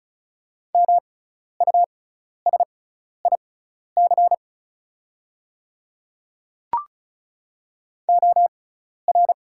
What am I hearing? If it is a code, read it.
Morse code, heard as a keyed tone, about 36 wpm, spaced as for 12 wpm, 700 Hz. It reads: MUSIC OR
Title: 6x Character Spacing - Top 500 Words - 35wpm